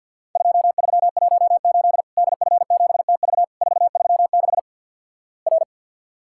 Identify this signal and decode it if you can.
Morse code, heard as a keyed tone, about 50 wpm, 700 Hz. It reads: 2318 DF7TV 436 R
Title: Worked All Europe DX Contest - QTC Exchange - 50wpm